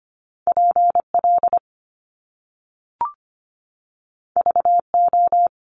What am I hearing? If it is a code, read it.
Morse code, heard as a keyed tone, about 25 wpm, 700 Hz. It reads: P L V O
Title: Top Two Letter Combinations in Words 25wpm